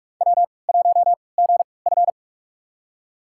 Morse code, heard as a keyed tone, about 46 wpm, spaced as for 26 wpm, 700 Hz. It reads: W1GF
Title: USA Call Signs - 3x Character Spacing - 45wpm